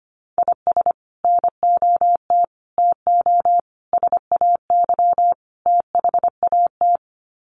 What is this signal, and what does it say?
Morse code, heard as a keyed tone, about 25 wpm, 700 Hz. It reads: IS NOT TO SAY THAT
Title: Sets of 5 Words 25wpm